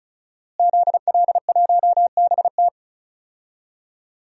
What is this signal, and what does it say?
Morse code, heard as a keyed tone, about 36 wpm, 700 Hz. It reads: ZL1BT
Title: DXCC-100 Call Signs - 5 Characters - Repeated 2x - 35wpm